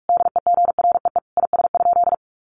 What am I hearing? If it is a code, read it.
Morse code, heard as a keyed tone, about 44 wpm, 700 Hz. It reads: DEGREE IS?
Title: Sentences from Top 1500 Words 45wpm